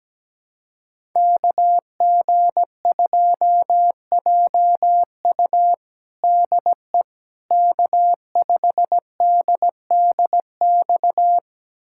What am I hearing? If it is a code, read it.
Morse code, heard as a keyed tone, about 17 wpm, 700 Hz. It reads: KG2JU DE K5DDX
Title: USA Callsign DE Callsign - 5 Characters - 17wpm